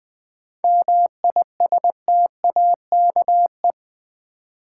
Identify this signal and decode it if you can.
Morse code, heard as a keyed tone, about 21 wpm, 700 Hz. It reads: MISTAKE
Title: Seven Letter Words - 20wpm